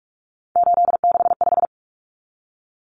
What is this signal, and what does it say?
Morse code, heard as a keyed tone, about 44 wpm, 700 Hz. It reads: 865
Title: Common Cut Numbers - 3-Digit 45wpm